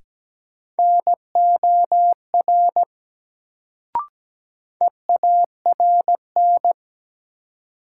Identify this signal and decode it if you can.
Morse code, heard as a keyed tone, about 17 wpm, 700 Hz. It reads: NOR EARN
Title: Sentences from Top 1500 Words - New Word Drill Alt - 17wpm